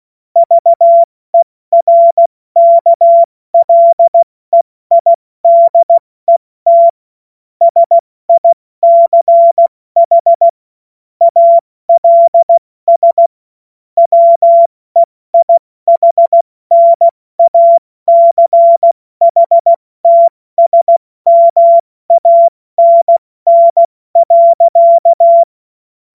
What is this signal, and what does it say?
Morse code, heard as a keyed tone, about 16 wpm, spaced as for 14 wpm, 680 Hz. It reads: VERKLEIDET SICH ALS WEIHNACHTSMANN.
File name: Morsekurs / Witze / Witz 14. März 2023 70 F80